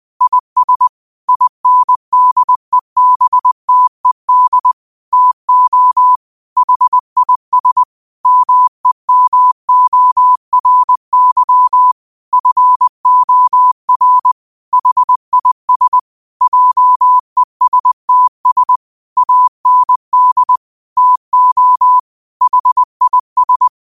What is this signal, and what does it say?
Morse code, heard as a keyed tone, about 20 wpm, 1000 Hz. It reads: IS INDEBTED TO HIS MEMORY FOR HIS JESTS AND TO HIS
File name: qotdfiles / 20wpm